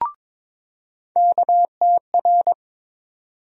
Morse code, heard as a keyed tone, about 22 wpm, 700 Hz. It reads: KTR
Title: USA Call Sign Postfixes - 3 Letters 22wpm